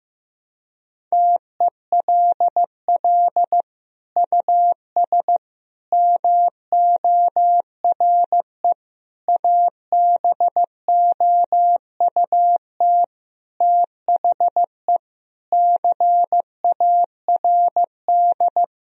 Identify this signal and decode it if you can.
Morse code, heard as a keyed tone, about 15 wpm, 700 Hz. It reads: TELL US MORE ABOUT THE CARD
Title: Sentences from Top 1000 Words - Review Part 4 - 15wpm